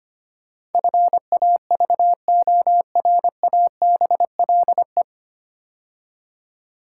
Morse code, heard as a keyed tone, about 25 wpm, 700 Hz. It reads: FAVORABLE
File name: practice / 25wpm / 9 Letter Words - First 3 Letters Given - 25wpm